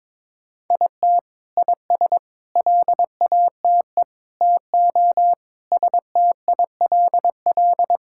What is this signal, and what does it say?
Morse code, heard as a keyed tone, about 22 wpm, 700 Hz. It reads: IT IS LATE TO STILL
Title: Sentences from Top 1000 Words - Review Part 5 - 22wpm